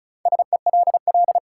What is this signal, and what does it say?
Morse code, heard as a keyed tone, about 36 wpm, 700 Hz. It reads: SELL
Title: Sentences from Top 1100 Words - New Word Drill Alt - 35wpm